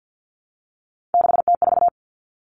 Morse code, heard as a keyed tone, about 50 wpm, 700 Hz. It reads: 6T4
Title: Three Challenging Letter-Number Combinations 50wpm